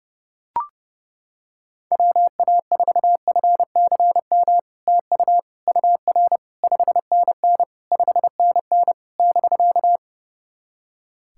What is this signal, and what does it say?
Morse code, heard as a keyed tone, about 30 wpm, 700 Hz. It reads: WA4FCM TU UR 5NN 5NN <BK>
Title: SOTA QSO Practice 30wpm